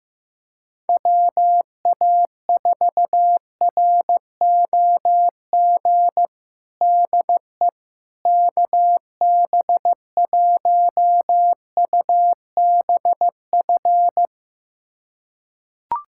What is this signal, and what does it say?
Morse code, heard as a keyed tone, about 16 wpm, 700 Hz. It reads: WA4ROG DE KB1UBF
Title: USA Callsign DE Callsign - 6 Characters - 15wpm